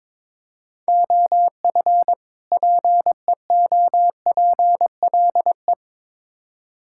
Morse code, heard as a keyed tone, about 22 wpm, 700 Hz. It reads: OF PEOPLE
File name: Build A Phrase - 2-Word Phrases - 22wpm